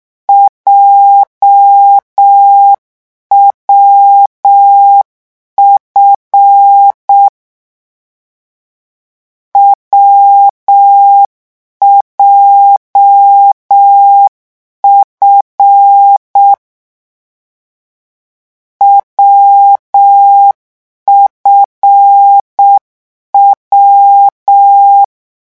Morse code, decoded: JWF WJF WFW